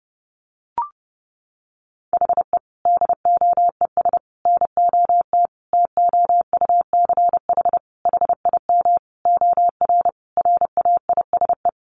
Morse code, heard as a keyed tone, about 28 wpm, 700 Hz. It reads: HE DOES NOT TOUCH HIM OR RAISE
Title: Sentences from Top 1400 Words 30wpm